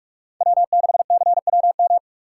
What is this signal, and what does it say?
Morse code, heard as a keyed tone, about 46 wpm, 700 Hz. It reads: W6CWM
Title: K1USN SST QSO Practice - 45wpm